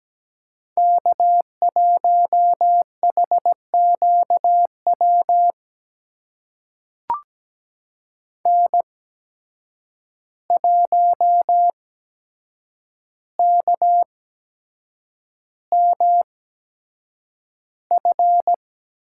Morse code, heard as a keyed tone, about 17 wpm, spaced as for 7 wpm, 700 Hz. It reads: K1HQW N1KMF
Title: USA Call Signs - 5 Characters - 8x Character Spacing - 17wpm